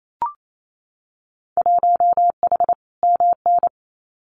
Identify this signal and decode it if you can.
Morse code, heard as a keyed tone, about 28 wpm, 700 Hz. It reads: 1H MN
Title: Winter Field Day Exchanges Encore 28wpm